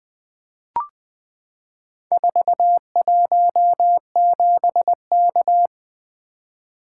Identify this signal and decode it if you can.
Morse code, heard as a keyed tone, about 20 wpm, 700 Hz. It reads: 417K
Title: Four Letter-Number Combinations - 20wpm